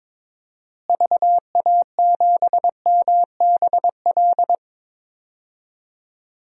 Morse code, heard as a keyed tone, about 22 wpm, 700 Hz. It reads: VA7MBL
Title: DXCC-100 Call Signs - 6 Characters - Repeated 2x - 22wpm